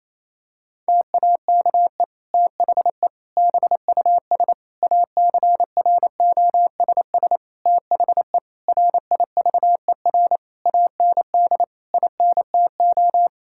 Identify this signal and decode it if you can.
Morse code, heard as a keyed tone, about 28 wpm, 700 Hz. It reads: TAKE THE BUS ACROSS THE RIVER AND INTO
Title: Sentences from Top 1000 Words - Review Part 6 - 28wpm